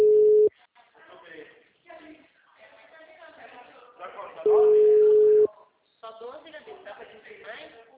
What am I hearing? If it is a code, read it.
Ringback tone (Europe, 425 Hz), with speech.